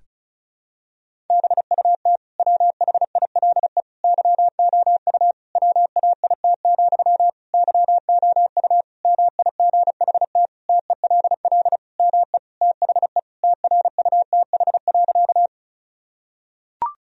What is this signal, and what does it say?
Morse code, heard as a keyed tone, about 34 wpm, 700 Hz. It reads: BUT WHILE YOU WAIT, YOU MIGHT TELL ME THE TRUTH.
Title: Sentences from Top 1000 Words - Review Part 1 - 35wpm